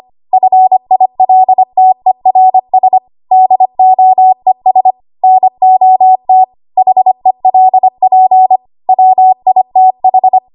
Morse code, heard as a keyed tone, about 25 wpm, 750 Hz. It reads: FILTERS DOES NOT HELP WITH
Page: W1AW Web Code Practice Files